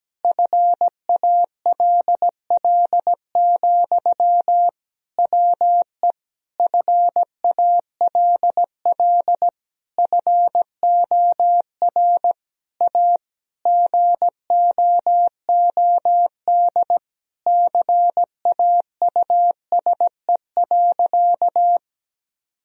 Morse code, heard as a keyed tone, about 17 wpm, 700 Hz. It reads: FALL, WE FALL FOR A GOOD CAUSE.
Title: Sentences from Top 1000 Words - Review Part 2 - 17wpm